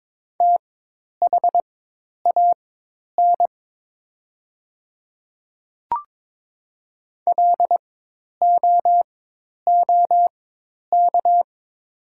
Morse code, heard as a keyed tone, about 22 wpm, spaced as for 10 wpm, 700 Hz. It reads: THAN LOOK
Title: 4x Character Spacing - Top 100 Words - 22wpm